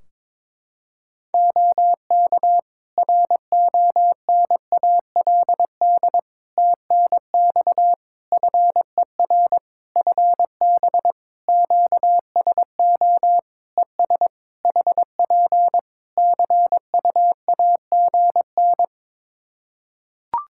Morse code, heard as a keyed tone, about 22 wpm, 700 Hz. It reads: OK RONALD TNX FER FB QSO ES HP CUAGN
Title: Practice QSO - Exchange 4 - Ending - 22wpm